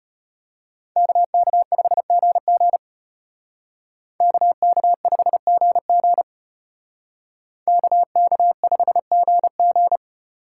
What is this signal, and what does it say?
Morse code, heard as a keyed tone, about 32 wpm, 700 Hz. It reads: KK5GG KK5GG KK5GG